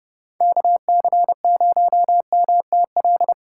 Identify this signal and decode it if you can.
Morse code, heard as a keyed tone, about 30 wpm, 700 Hz. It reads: KC0MTL
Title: USA Call Signs - 2x Character Spacing - 30wpm